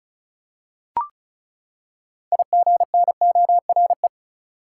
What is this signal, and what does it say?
Morse code, heard as a keyed tone, about 34 wpm, 700 Hz. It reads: IGNORE
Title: Six Letter Words - 35wpm